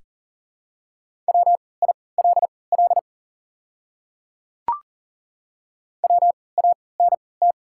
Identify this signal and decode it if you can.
Morse code, heard as a keyed tone, about 40 wpm, spaced as for 23 wpm, 700 Hz. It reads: WILL WANT
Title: 3x Character Spacing - Top 300 Words - 40wpm